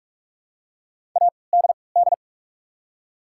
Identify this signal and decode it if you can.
Morse code, heard as a keyed tone, about 43 wpm, spaced as for 25 wpm, 700 Hz. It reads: ADD
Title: 3x Character Spacing - Top 300 Words - 45wpm